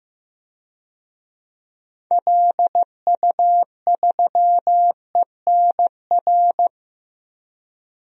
Morse code, heard as a keyed tone, about 15 wpm, 700 Hz. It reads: LU3ENR
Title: DXCC-100 Call Signs - 6 Characters - 15wpm